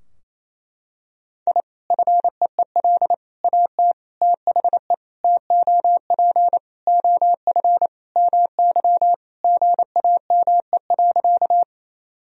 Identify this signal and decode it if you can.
Morse code, heard as a keyed tone, about 28 wpm, 700 Hz. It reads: I FEEL AT THE TOP OF MY GAME.
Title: Sentences from Top 1000 Words - Review Part 2 - 28wpm